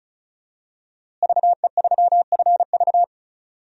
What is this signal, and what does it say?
Morse code, heard as a keyed tone, about 34 wpm, 700 Hz. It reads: VE3FV
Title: DXCC-100 Call Signs - 5 Characters - Repeated 3x - 35wpm